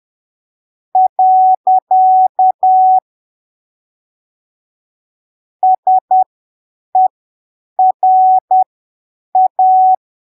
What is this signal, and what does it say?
Morse code, heard as a keyed tone, about 10 wpm, spaced as for 7 wpm, 750 Hz. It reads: . SERA
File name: arquivos / texto / texto 25 10ppm